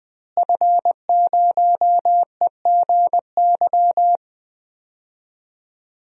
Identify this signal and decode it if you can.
Morse code, heard as a keyed tone, about 21 wpm, 700 Hz. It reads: F0EGY
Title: DXCC-100 Call Signs - 5 Characters - Repeated 2x - 20wpm